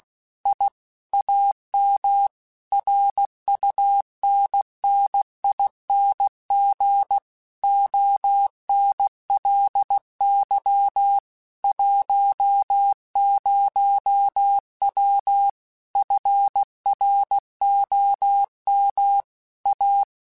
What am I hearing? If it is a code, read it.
Morse code, heard as a keyed tone, about 16 wpm, 800 Hz. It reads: I AM RUNNING ONLY 10W FROM A
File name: QSOs / QSOs-mp3s / kc7haa@15